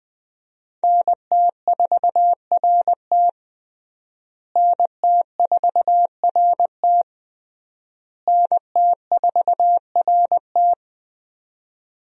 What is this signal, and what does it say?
Morse code, heard as a keyed tone, about 20 wpm, 700 Hz. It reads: NT4RT NT4RT NT4RT
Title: USA Call Signs - 5 Characters - Repeated 3x - 20wpm